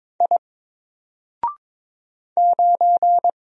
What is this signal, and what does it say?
Morse code, heard as a keyed tone, about 22 wpm, 700 Hz. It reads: I 9
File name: Single-Letter-Number-22wpm